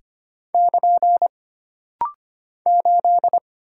Morse code, heard as a keyed tone, about 25 wpm, 700 Hz. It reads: <KN> 8
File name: Single Character (Letter:Number:Punctuation:Prosign) - Rapid-Fire - 25wpm